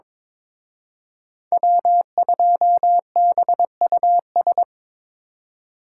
Morse code, heard as a keyed tone, about 25 wpm, 700 Hz. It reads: W2BUS
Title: Speed Racing - USA 1x3 Call Signs 22wpm